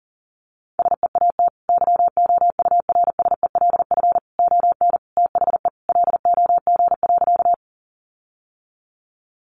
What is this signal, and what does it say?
Morse code, heard as a keyed tone, about 37 wpm, 700 Hz. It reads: SEAT YOURSELF ON THE LOG.
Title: Sentences from Top 1500 Words 40wpm